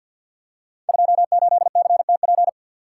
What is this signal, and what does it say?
Morse code, heard as a keyed tone, about 49 wpm, 700 Hz. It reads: 28CTP
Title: Code Groups - 5 Letter:Numbers - 50wpm